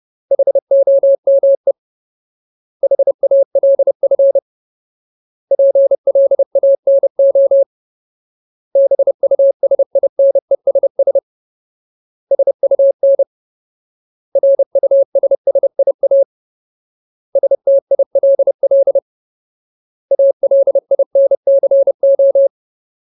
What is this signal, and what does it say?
Morse code, heard as a keyed tone, about 30 wpm, 550 Hz. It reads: HOME HALF PLANO BUSINESS SUN RUSSIA STILL ALINCO
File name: Ultimate Word List / 30WPM Ultimate